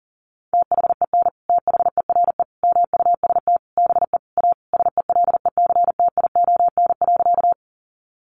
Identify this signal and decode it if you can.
Morse code, heard as a keyed tone, about 40 wpm, 700 Hz. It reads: THEN THERE MUST BE A SELECTION.
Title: Sentences from Top 1400 Words 40wpm